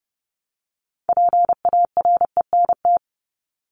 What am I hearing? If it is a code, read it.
Morse code, heard as a keyed tone, about 30 wpm, 700 Hz. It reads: PARENT
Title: ES:S Suffix - Top 100 Words with Base Words 30wpm